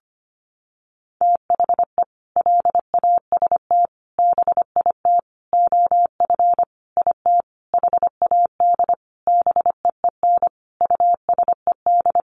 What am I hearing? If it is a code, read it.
Morse code, heard as a keyed tone, about 25 wpm, 700 Hz. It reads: THE LAST BIT OF IT HAD BEEN USED
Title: Sentences from Top 900 Words 25wpm